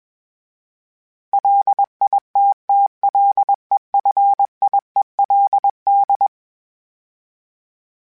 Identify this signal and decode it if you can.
Morse code, heard as a keyed tone, about 21 wpm, 800 Hz. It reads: LITTLEFIELD
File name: fromSMs / FromSMs mp3s / Cities@20